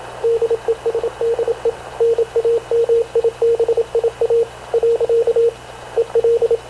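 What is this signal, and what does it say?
Morse code, heard as a keyed tone, about 26 wpm, 460 Hz, with radio noise. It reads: DESDE NAMIBIA . EL
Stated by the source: Havana Radio / CLA sending Spanish press (PX) in Morse, recorded by Radio Officer Paul Dumesnil onboard CNAV QUEST/CZDO.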